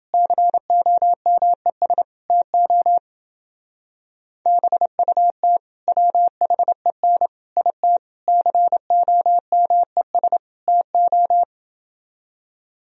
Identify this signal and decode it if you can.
Morse code, heard as a keyed tone, about 27 wpm, 700 Hz. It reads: COMES TO BUT WHEN IT COMES TO